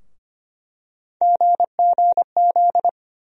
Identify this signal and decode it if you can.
Morse code, heard as a keyed tone, about 25 wpm, 700 Hz. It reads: GGZ
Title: USA Call Sign Postfixes - 3 Letters 25wpm